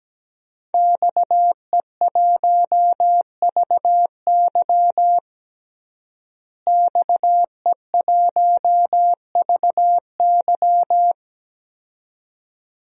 Morse code, heard as a keyed tone, about 17 wpm, 700 Hz. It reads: XE1VY XE1VY
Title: DXCC-100 Call Signs - 5 Characters - Repeated 2x - 17wpm